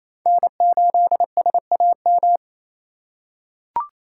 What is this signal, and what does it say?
Morse code, heard as a keyed tone, about 28 wpm, 700 Hz. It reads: N8SAM
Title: USA Call Signs - 5 Characters - 2x Character Spacing - 28wpm